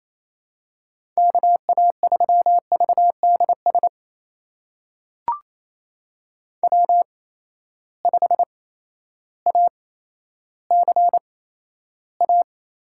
Morse code, heard as a keyed tone, about 28 wpm, spaced as for 13 wpm, 700 Hz. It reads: KA3VDS W5ACA
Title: USA Call Signs - 8x Character Spacing - 28wpm